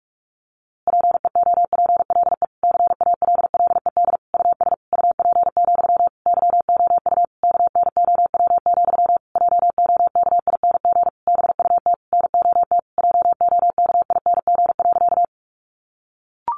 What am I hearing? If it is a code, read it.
Morse code, heard as a keyed tone, about 45 wpm, 700 Hz. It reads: PEOPLE CALLED US UP, YOU KNOW, JOKING BUT NOT JOKING.
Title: A Sound Of Thunder 45wpm